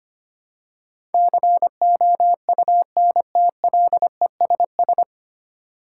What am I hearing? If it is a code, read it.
Morse code, heard as a keyed tone, about 25 wpm, 700 Hz. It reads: COUNTLESS
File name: Nine Letter Words - 25wpm